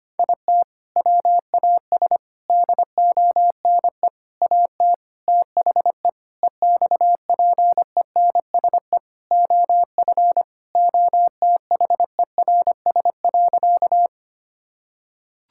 Morse code, heard as a keyed tone, about 25 wpm, 700 Hz. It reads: IT WAS DONE AT THE EXPENSE OF OTHERS.
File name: practice / 25wpm / Sentences with 4-Word Phrases - 25wpm